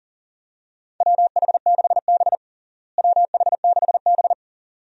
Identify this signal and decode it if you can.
Morse code, heard as a keyed tone, about 40 wpm, 700 Hz. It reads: WH6B WH6B
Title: USA Call Signs - 4 Characters - Repeated 2x - 40wpm